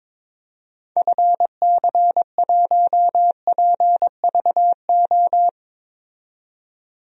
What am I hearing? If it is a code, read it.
Morse code, heard as a keyed tone, about 21 wpm, 700 Hz. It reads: FC1PVO
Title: DXCC-100 Call Signs - 6 Characters - 22wpm